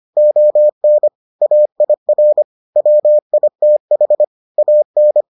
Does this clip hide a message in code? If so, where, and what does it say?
Yes: Morse code from 0.2 to 5.2 s, heard as a keyed tone, about 25 wpm, 600 Hz. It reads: ON AIR WITH AN